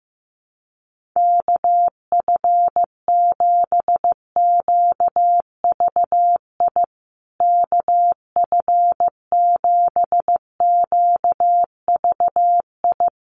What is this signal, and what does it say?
Morse code, heard as a keyed tone, about 15 wpm, 700 Hz. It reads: KF7QVI KF7QVI
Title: USA Call Signs Repeated 3x 15wpm